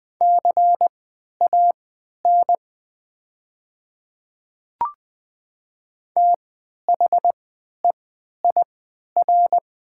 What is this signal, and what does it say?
Morse code, heard as a keyed tone, about 20 wpm, spaced as for 11 wpm, 700 Hz. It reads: CAN THEIR
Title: 3x Character Spacing - Top 100 Words - 20wpm